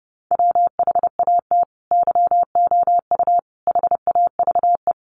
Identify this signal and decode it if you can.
Morse code, heard as a keyed tone, about 30 wpm, 700 Hz. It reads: WHAT YOU HAVE
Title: Sets of 5 Words Encore 30wpm